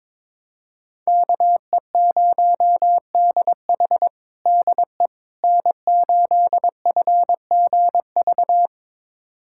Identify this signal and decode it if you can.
Morse code, heard as a keyed tone, about 22 wpm, 700 Hz. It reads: KE0DH DE N8FGV
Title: USA Callsign DE Callsign - 5 Characters - 22wpm